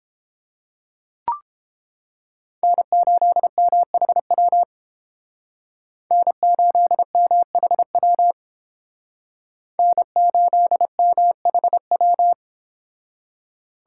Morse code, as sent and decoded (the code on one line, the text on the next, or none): -. ---.. -- .... .-- / -. ---.. -- .... .-- / -. ---.. -- .... .--
N8MHW N8MHW N8MHW